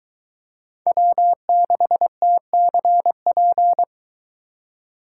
Morse code, heard as a keyed tone, about 23 wpm, 700 Hz. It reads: W6TCP